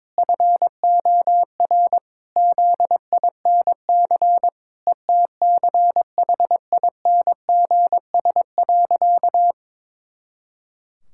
Morse code, heard as a keyed tone, about 22 wpm, 700 Hz. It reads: FOR ZINC ETCHINGS.
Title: Pangram Challenge v2 - 22wpm